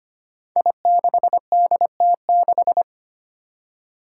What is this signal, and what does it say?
Morse code, heard as a keyed tone, about 25 wpm, 700 Hz. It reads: I6DT6
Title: Five Challenging Letter-Number Combinations 25wpm